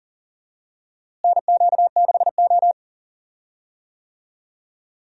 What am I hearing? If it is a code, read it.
Morse code, heard as a keyed tone, about 39 wpm, 700 Hz. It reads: NQ6O
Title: USA Call Signs - 4 Characters - Repeated 3x - 40wpm